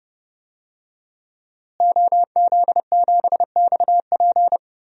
Morse code, heard as a keyed tone, about 32 wpm, 700 Hz. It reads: OZ7XP
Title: DXCC-100 Call Signs - 5 Characters - Repeated 2x - 30wpm